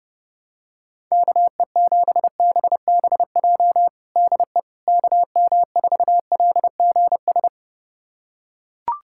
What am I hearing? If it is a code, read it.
Morse code, heard as a keyed tone, about 30 wpm, 700 Hz. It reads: KE7BBJ DE KM4LGS
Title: USA Callsign DE Callsign 30wpm